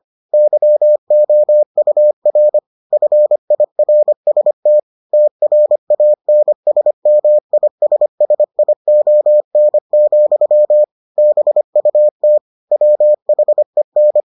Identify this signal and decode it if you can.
Morse code, heard as a keyed tone, about 25 wpm, 600 Hz. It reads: YOUR FIRST TRANSMISSION, BUT WHEN